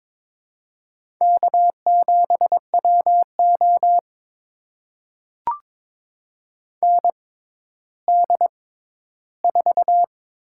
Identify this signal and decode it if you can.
Morse code, heard as a keyed tone, about 22 wpm, spaced as for 12 wpm, 700 Hz. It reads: K7WO ND4
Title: USA Call Signs - 4 Characters - 6x Character Spacing - 22wpm